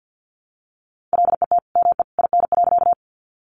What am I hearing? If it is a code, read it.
Morse code, heard as a keyed tone, about 49 wpm, 700 Hz. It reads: LET ME IN.
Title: Sentences from Top 300 Words 50wpm